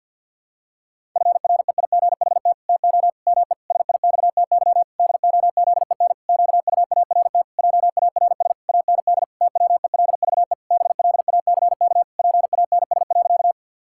Morse code, heard as a keyed tone, about 52 wpm, 700 Hz. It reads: FREIGHT TO ME SIXTY DOZEN QUART JARS AND TWELVE BLACK PANS.
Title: Pangram Challenge v2 - 50wpm